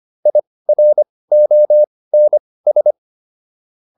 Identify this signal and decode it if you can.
Morse code, heard as a keyed tone, about 25 wpm, spaced as for 18 wpm, 600 Hz. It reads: IRONS